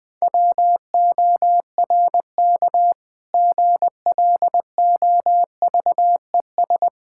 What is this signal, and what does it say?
Morse code, heard as a keyed tone, about 20 wpm, 690 Hz. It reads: WORK GLOVES
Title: Difficult Words with Hints - 20wpm